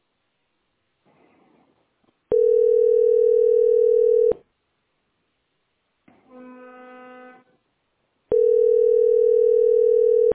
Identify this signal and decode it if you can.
Ringback tone (North America), 440 + 480 Hz.